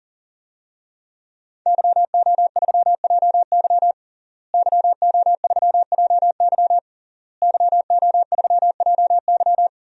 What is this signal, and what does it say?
Morse code, heard as a keyed tone, about 40 wpm, 700 Hz. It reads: YO3JY YO3JY YO3JY
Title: DXCC-100 Call Signs - 5 Characters - Repeated 3x - 40wpm